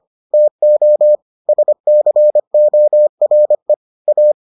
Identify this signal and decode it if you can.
Morse code, heard as a keyed tone, about 25 wpm, 600 Hz. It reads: TO SCORE A